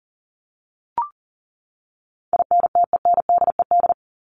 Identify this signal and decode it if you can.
Morse code, heard as a keyed tone, about 40 wpm, 700 Hz. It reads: INTENDED
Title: ED Suffix - Top 100 Words with Base Words 40wpm